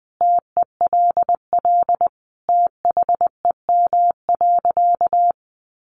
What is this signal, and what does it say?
Morse code, heard as a keyed tone, about 19 wpm, 700 Hz. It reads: TELL THEM.
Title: Sentences from Top 700 Words 20wpm